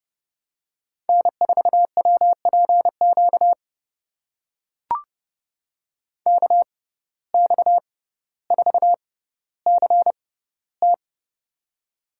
Morse code, heard as a keyed tone, about 32 wpm, spaced as for 16 wpm, 700 Hz. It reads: N4WPQ KX4CT
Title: USA Call Signs - 5 Characters - 6x Character Spacing - 30wpm